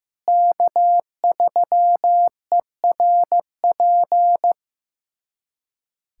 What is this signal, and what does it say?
Morse code, heard as a keyed tone, about 16 wpm, 700 Hz. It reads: K3ERP
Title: USA Callsign DE Callsign 15wpm